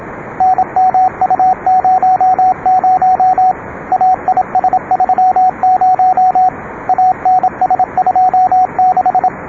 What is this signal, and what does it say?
Morse code, heard as a keyed tone, about 26 wpm, 780 Hz, with radio noise. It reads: NMU00 AIS30 ANS26